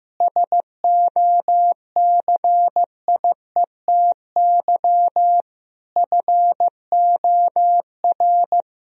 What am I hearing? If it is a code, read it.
Morse code, heard as a keyed tone, about 15 wpm, 700 Hz. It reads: SOCIETY FOR
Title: Sentences from Top 1000 Words - Review Part 4 - 15wpm